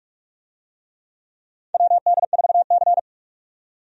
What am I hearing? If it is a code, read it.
Morse code, heard as a keyed tone, about 46 wpm, 700 Hz. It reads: WD4C
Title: USA Call Signs - 4 Characters - 45wpm